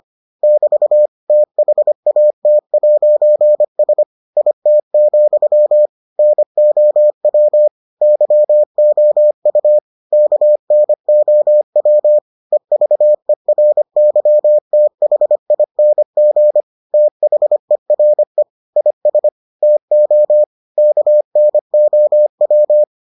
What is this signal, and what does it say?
Morse code, heard as a keyed tone, about 25 wpm, 600 Hz. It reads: <BT> THAT'S IT, NOW YOU KNOW EVERYTHING THERE IS TO KNOW